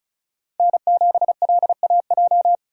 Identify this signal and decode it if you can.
Morse code, heard as a keyed tone, about 34 wpm, 700 Hz. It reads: N7LAJ
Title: USA Call Signs - 5 Characters - 4x Character Spacing - 35wpm